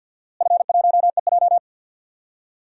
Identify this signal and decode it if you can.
Morse code, heard as a keyed tone, about 51 wpm, 700 Hz. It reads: F1EJ
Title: DXCC-100 Call Signs - 4 Characters - Repeated 2x - 50wpm